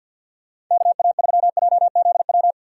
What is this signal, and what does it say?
Morse code, heard as a keyed tone, about 52 wpm, 700 Hz. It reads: KA3JZW
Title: USA Call Signs - 6 Characters - 6x Character Spacing - 50wpm